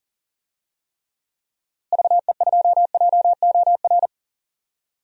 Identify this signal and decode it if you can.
Morse code, heard as a keyed tone, about 40 wpm, 700 Hz. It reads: VE2JOR
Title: DXCC-100 Call Signs - 6 Characters - Repeated 2x - 40wpm